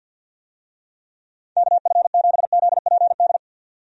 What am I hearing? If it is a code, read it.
Morse code, heard as a keyed tone, about 51 wpm, 700 Hz. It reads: KF7ZPD
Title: USA Call Signs - 6 Characters - 50wpm